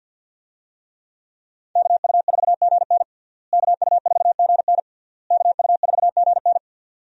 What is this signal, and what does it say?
Morse code, heard as a keyed tone, about 46 wpm, 700 Hz. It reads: KU4GN KU4GN KU4GN
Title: USA Call Signs - 5 Characters - Repeated 3x - 50wpm